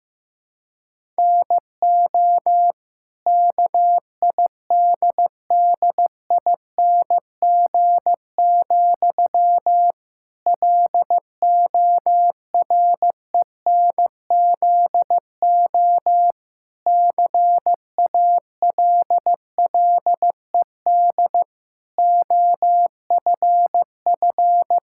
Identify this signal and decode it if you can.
Morse code, heard as a keyed tone, about 15 wpm, 700 Hz. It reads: NO KIDDING, LORENZO CALLED OFF
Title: Pangram Challenge v2 - 15wpm